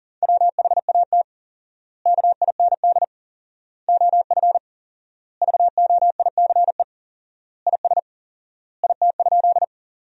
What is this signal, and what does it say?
Morse code, heard as a keyed tone, about 40 wpm, 700 Hz. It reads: WHAT KIND OF VOICE IS IT?
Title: 4x Word-Spacing - Sentences from Top 500 Words - 40wpm